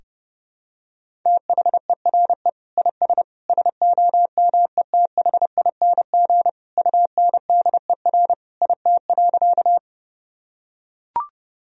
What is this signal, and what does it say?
Morse code, heard as a keyed tone, about 30 wpm, 700 Hz. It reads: THERE IS SOMETHING UNDER IT.
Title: Sentences from Top 1000 Words - Review Part 1 - 30wpm